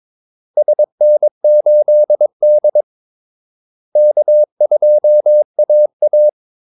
Morse code, heard as a keyed tone, about 23 wpm, 600 Hz. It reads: SN8D K2AA
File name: Callsigns-22wpm